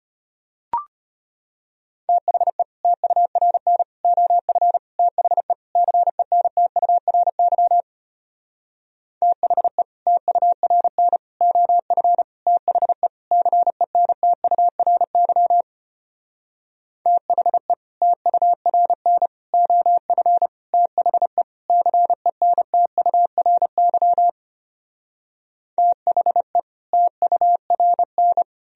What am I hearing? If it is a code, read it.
Morse code, heard as a keyed tone, about 33 wpm, 700 Hz. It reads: THE TURN OF THE CENTURY THE TURN OF THE CENTURY THE TURN OF THE CENTURY THE TURN